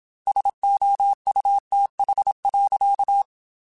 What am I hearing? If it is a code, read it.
Morse code, heard as a keyed tone, about 27 wpm, 800 Hz. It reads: SOUTH.